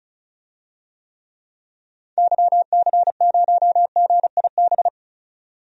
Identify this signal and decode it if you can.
Morse code, heard as a keyed tone, about 36 wpm, 700 Hz. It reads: YC0GIB
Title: DXCC-100 Call Signs - 6 Characters - 35wpm